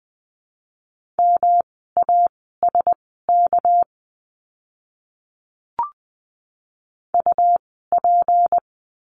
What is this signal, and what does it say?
Morse code, heard as a keyed tone, about 20 wpm, spaced as for 14 wpm, 700 Hz. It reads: MASK UP
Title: 199 Lesson 32f - Letter k - Words 20wpm